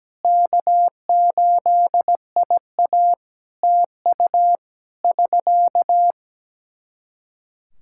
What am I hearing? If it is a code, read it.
Morse code, heard as a keyed tone, about 17 wpm, 700 Hz. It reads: K8IA TU <SK>
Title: K1USN SST QSO Practice 17wpm